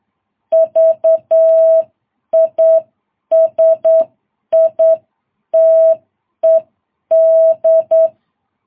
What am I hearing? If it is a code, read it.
Morse code, heard as a keyed tone, about 8 wpm, 650 Hz. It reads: VISITED
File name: jh8cbh